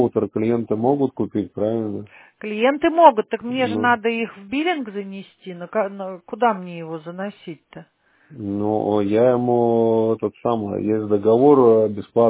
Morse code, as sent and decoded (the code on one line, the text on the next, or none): none